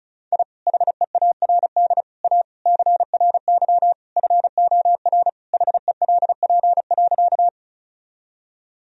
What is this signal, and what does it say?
Morse code, heard as a keyed tone, about 34 wpm, 700 Hz. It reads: I HEARD A CRY FOR HELP.
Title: Sentences from Top 1000 Words - Review Part 2 - 35wpm